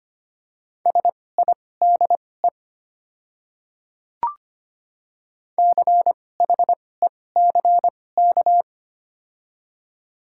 Morse code, heard as a keyed tone, about 25 wpm, spaced as for 18 wpm, 700 Hz. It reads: SIDE CHECK
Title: 2x Character Spacing - Top 500 Words - 25wpm